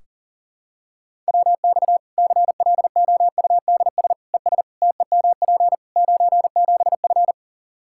Morse code, heard as a keyed tone, about 40 wpm, 700 Hz. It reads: WX CLOUDS ES TEMP 97F
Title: Practice QSO - Exchange 2 - Info - 40wpm